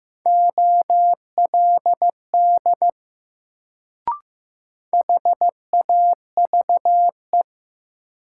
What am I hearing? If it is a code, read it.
Morse code, heard as a keyed tone, about 15 wpm, 700 Hz. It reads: OLD HAVE
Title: Sets of 2 Words - New Word Drill - 15wpm